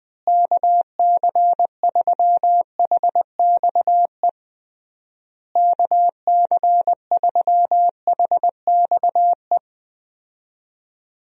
Morse code, heard as a keyed tone, about 20 wpm, 700 Hz. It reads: KC3HXE KC3HXE
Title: USA Call Signs - 6 Characters - Repeated 3x - 20wpm